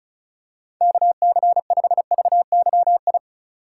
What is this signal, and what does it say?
Morse code, heard as a keyed tone, about 34 wpm, 700 Hz. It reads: KC5VYI
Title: USA Call Signs - 6 Characters - 4x Character Spacing - 35wpm